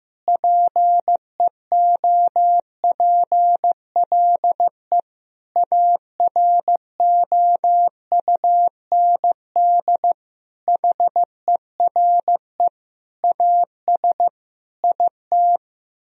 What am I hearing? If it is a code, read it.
Morse code, heard as a keyed tone, about 15 wpm, 700 Hz. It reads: PEOPLE AROUND HERE AS IT
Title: Sentences from Top 1000 Words - Review Part 3 - 15wpm